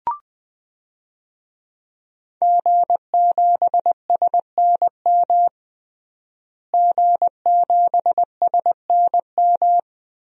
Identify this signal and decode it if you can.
Morse code, heard as a keyed tone, about 20 wpm, 700 Hz. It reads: G7SNM G7SNM
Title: DXCC-100 Call Signs - 5 Characters - Repeated 2x - 20wpm